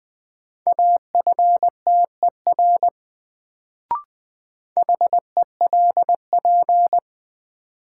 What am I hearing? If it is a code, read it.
Morse code, heard as a keyed tone, about 20 wpm, 700 Hz. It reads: AFTER HELP
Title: Sets of 2 Words - New Word Drill - 20wpm